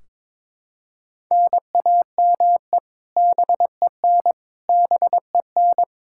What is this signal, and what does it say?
Morse code, heard as a keyed tone, about 21 wpm, 700 Hz. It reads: NAME BEN BEN
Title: K1USN SST QSO Practice 22wpm